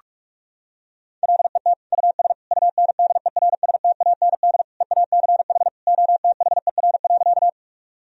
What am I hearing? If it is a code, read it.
Morse code, heard as a keyed tone, about 46 wpm, 700 Hz. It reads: LET US UNDERSTAND EACH OTHER.
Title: Sentences from Top 1000 Words - Review Part 1 - 45wpm